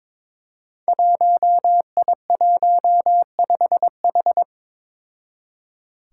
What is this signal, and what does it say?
Morse code, heard as a keyed tone, about 22 wpm, 700 Hz. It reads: 1I15H
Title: Five Challenging Letter-Number Combinations 22wpm